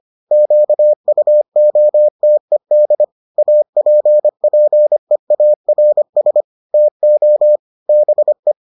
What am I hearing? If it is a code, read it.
Morse code, heard as a keyed tone, about 25 wpm, 600 Hz. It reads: QUOTED APPEARS TO BE